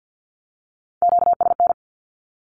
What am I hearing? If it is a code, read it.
Morse code, heard as a keyed tone, about 50 wpm, 700 Hz. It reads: QSN
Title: Q-Codes with Spoken Meaning - 50wpm